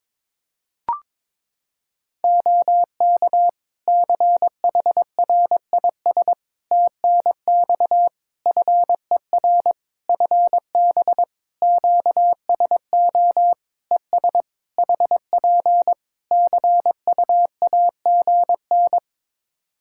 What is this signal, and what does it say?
Morse code, heard as a keyed tone, about 22 wpm, 700 Hz. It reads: OK CHRIS TNX FER FB QSO ES HP CUAGN
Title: Practice QSO - Exchange 4 - Ending - 22wpm